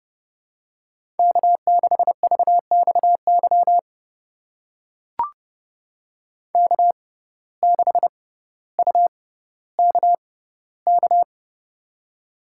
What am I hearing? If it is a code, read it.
Morse code, heard as a keyed tone, about 30 wpm, spaced as for 15 wpm, 700 Hz. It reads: K6VXY K6UKK
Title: USA Call Signs - 5 Characters - 6x Character Spacing - 30wpm